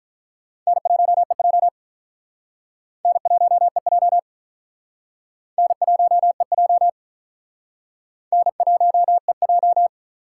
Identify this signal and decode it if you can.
Morse code, heard as a keyed tone, about 45 wpm, 700 Hz. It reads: N1EJ N1EJ N1EJ N1EJ